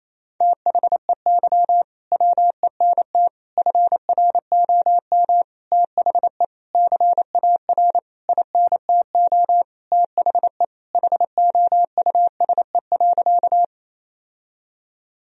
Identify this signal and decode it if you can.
Morse code, heard as a keyed tone, about 28 wpm, 700 Hz. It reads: THEY WENT FROM THE CAR INTO THE HOUSE.
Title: Sentences from Top 1000 Words - Review Part 1 - 28wpm